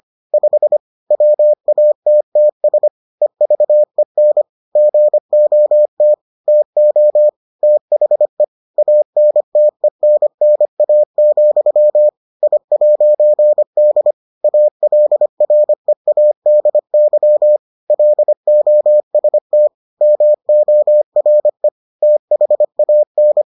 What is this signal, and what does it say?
Morse code, heard as a keyed tone, about 25 wpm, 600 Hz. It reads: 5 WATTS EVEN GOT TO THE ANTENNA, I'D ALREADY LOST MORE THAN